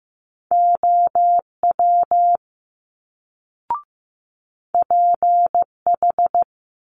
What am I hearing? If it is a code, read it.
Morse code, heard as a keyed tone, about 15 wpm, 700 Hz. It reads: OW PH
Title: Multi-letter Phonemes Quick Fire 15wpm